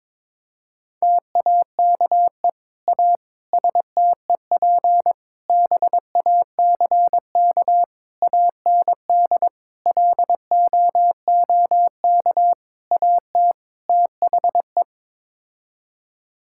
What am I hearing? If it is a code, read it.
Morse code, heard as a keyed tone, about 22 wpm, 700 Hz. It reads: TAKE A STEP BACK AND LOOK AT THE
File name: practice / 22wpm / Sets of 8 Words - 22wpm